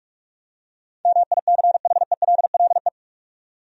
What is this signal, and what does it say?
Morse code, heard as a keyed tone, about 46 wpm, 700 Hz. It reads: MICHELLE
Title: USA Common Names - 45wpm